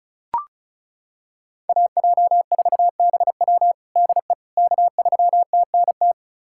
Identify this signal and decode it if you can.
Morse code, heard as a keyed tone, about 36 wpm, 700 Hz. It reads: AJ4BW DE K3TNT
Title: USA Callsign DE Callsign - 5 Characters - 35wpm